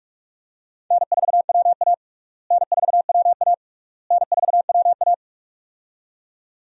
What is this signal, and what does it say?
Morse code, heard as a keyed tone, about 42 wpm, 700 Hz. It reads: N4WA N4WA N4WA
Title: USA Call Signs - 4 Characters - Repeated 3x - 45wpm